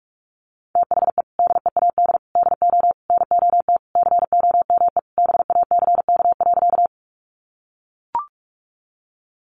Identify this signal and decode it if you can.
Morse code, heard as a keyed tone, about 44 wpm, 700 Hz. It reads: THE DEAD DO NOT COME BACK.
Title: Sentences from Top 1500 Words 45wpm